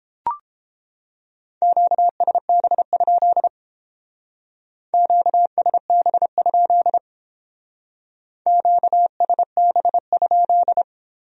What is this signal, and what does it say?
Morse code, heard as a keyed tone, about 30 wpm, 700 Hz. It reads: QSB? QSB? QSB?